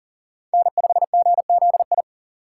Morse code, heard as a keyed tone, about 40 wpm, 700 Hz. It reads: N5GZI
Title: USA Call Signs - 5 Characters - 3x Character Spacing - 40wpm